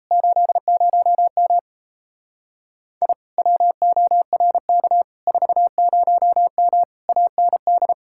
Morse code, heard as a keyed tone, about 36 wpm, 700 Hz. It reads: 80M I WORK 40M AND